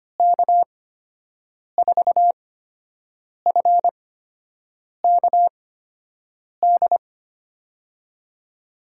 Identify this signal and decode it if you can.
Morse code, heard as a keyed tone, about 25 wpm, spaced as for 7 wpm, 700 Hz. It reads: K4FKD
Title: USA Call Signs - 6 Characters - 8x Character Spacing - 25wpm